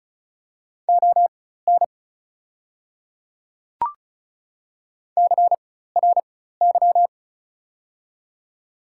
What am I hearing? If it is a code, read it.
Morse code, heard as a keyed tone, about 36 wpm, spaced as for 16 wpm, 690 Hz. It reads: ON CRY
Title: 4x Character Spacing - Top 500 Words - 35wpm